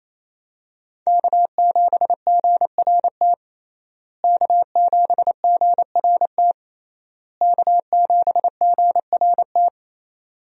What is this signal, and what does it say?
Morse code, heard as a keyed tone, about 28 wpm, 700 Hz. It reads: K7GRT K7GRT K7GRT
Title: USA Call Signs - 5 Characters - Repeated 3x - 28wpm